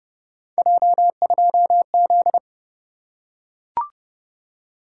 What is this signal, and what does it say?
Morse code, heard as a keyed tone, about 29 wpm, 700 Hz. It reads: J2Z